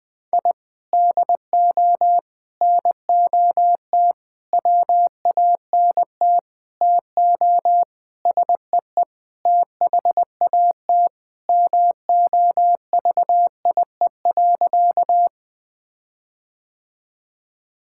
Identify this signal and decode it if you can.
Morse code, heard as a keyed tone, about 20 wpm, 700 Hz. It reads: I DO NOT WANT TO SEE THAT MOVIE.
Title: Sentences with 5-Word Phrases - Spoken Beginning - 20wpm